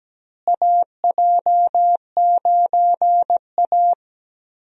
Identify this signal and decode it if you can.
Morse code, heard as a keyed tone, about 17 wpm, 700 Hz. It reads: AJ9A
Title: USA Call Signs - 4 Characters - Repeated 3x - 17wpm